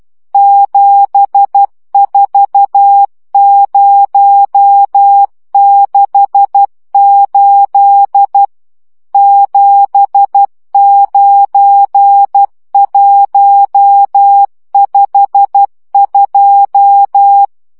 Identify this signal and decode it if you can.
Morse code, heard as a keyed tone, about 12 wpm, 800 Hz. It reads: 74068 79152
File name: morze vizsga 2011-3 10